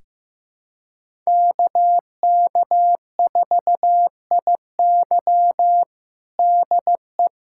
Morse code, heard as a keyed tone, about 15 wpm, 700 Hz. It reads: KK4IY DE
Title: USA Callsign DE Callsign - 5 Characters - 15wpm